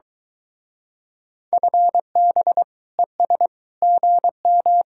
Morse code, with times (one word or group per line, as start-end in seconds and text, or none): 1.5-2.6 FB
3.0-3.5 ES
3.8-4.8 GM